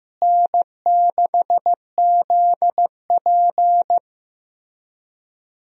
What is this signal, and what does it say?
Morse code, heard as a keyed tone, about 15 wpm, 700 Hz. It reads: N6ZP
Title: USA Call Signs - 4 Characters - Repeated 3x - 15wpm